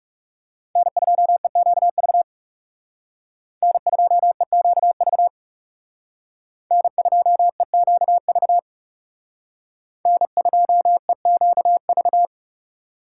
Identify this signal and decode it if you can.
Morse code, heard as a keyed tone, about 40 wpm, 700 Hz. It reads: N2EQV N2EQV N2EQV N2EQV